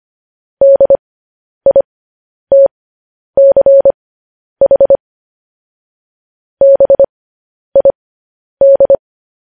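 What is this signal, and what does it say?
Morse code, heard as a keyed tone, about 25 wpm, spaced as for 10 wpm, 550 Hz. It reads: DITCH BID
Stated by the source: Morse code news headlines sent at 10 wpm.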